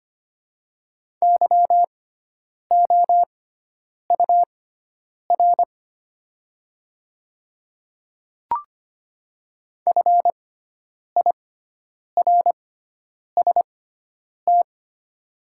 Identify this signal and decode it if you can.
Morse code, heard as a keyed tone, about 25 wpm, spaced as for 9 wpm, 700 Hz. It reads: YOUR FIRST
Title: 6x Character Spacing - Top 100 Words - 25wpm